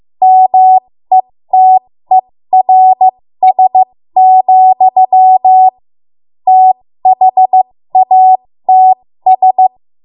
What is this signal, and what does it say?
Morse code, heard as a keyed tone, about 15 wpm, spaced as for 13 wpm, 750 Hz. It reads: METERS, THATS
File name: Morse / 13WPM